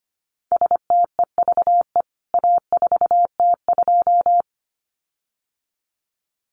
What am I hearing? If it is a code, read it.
Morse code, heard as a keyed tone, about 25 wpm, 700 Hz. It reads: STEVE A4T2
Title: CWT Exchange 25wpm